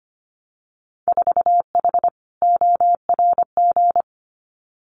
Morse code, heard as a keyed tone, about 25 wpm, 700 Hz. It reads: 4H ORG
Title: Winter Field Day Exchanges Encore 25wpm